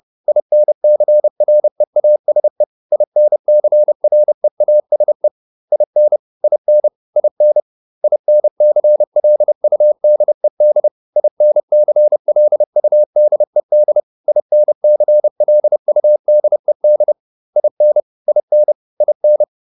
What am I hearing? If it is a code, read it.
Morse code, heard as a keyed tone, about 30 wpm, 600 Hz. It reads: INCREASE INCREASE IN IN IN INCLUDED INCLUDED INCLUDED IN IN IN